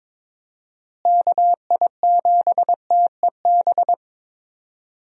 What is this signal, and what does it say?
Morse code, heard as a keyed tone, about 22 wpm, 700 Hz. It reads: KI7TEB
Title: USA Call Signs - 6 Characters - Repeated 3x - 22wpm